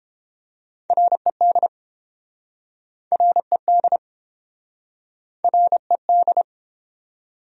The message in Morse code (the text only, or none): RED RED RED